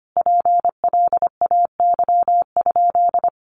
Morse code, heard as a keyed tone, about 25 wpm, 700 Hz. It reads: PLAY?
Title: Sentences from Top 1400 Words 25wpm